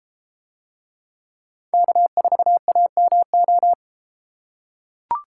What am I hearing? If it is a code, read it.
Morse code, heard as a keyed tone, about 31 wpm, 700 Hz. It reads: K4AMO